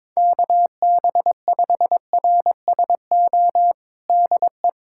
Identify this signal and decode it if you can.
Morse code, heard as a keyed tone, about 22 wpm, 700 Hz. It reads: KB5RSO DE
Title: USA Callsign DE Callsign - 6 Characters - 22wpm